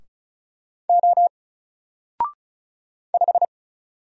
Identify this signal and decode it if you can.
Morse code, heard as a keyed tone, about 35 wpm, 700 Hz. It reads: O 5
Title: Single Character (Letter:Number:Punctuation:Prosign) 35wpm